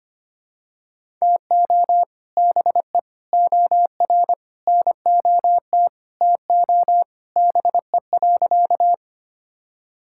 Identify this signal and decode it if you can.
Morse code, heard as a keyed tone, about 25 wpm, 700 Hz. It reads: TO BE OR NOT TO BE.
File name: practice / 25wpm / Sentences with 2-Word Phrases - 25wpm